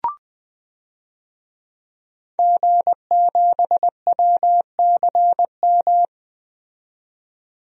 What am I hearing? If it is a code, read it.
Morse code, heard as a keyed tone, about 20 wpm, 700 Hz. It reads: G7WCM
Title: DXCC-100 Call Signs - 5 Characters - 20wpm